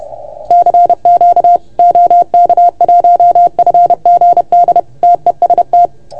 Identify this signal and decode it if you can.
Morse code, heard as a keyed tone, about 30 wpm, 690 Hz. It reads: CQ OK1FGD TEST
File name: Archiv / CUC381 / OK1FGD CUC 381